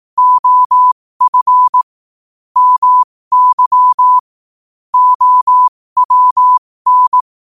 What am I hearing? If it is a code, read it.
Morse code, heard as a keyed tone, about 18 wpm, spaced as for 15 wpm, 1000 Hz. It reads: OF MY OWN